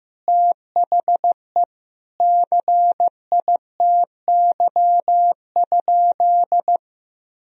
Morse code, heard as a keyed tone, about 15 wpm, 700 Hz. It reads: THE CITY?
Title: Sentences from Top 1000 Words - Review Part 1 - 15wpm